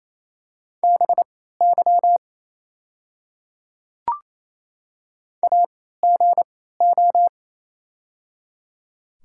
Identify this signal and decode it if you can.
Morse code, heard as a keyed tone, about 28 wpm, spaced as for 16 wpm, 700 Hz. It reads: BY AGO
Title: 3x Character Spacing - Top 500 Words - 28wpm